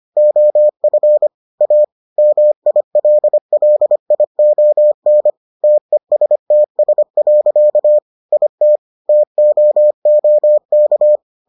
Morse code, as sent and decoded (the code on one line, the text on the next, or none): --- ..-. / .- / -- .. .-.. .-.. .. --- -. / - . ... - ... .-.-.- / .. - / - --- --- -.-
OF A MILLION TESTS. IT TOOK